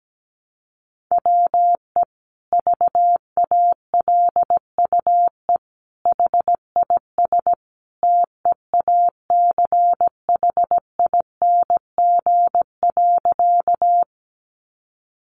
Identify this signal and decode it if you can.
Morse code, heard as a keyed tone, about 17 wpm, 700 Hz. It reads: WE VALUE HIS TEACHING.
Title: Sentences from Top 1100 Words 17wpm